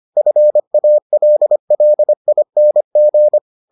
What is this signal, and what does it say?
Morse code, heard as a keyed tone, about 25 wpm, 600 Hz. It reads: FALLING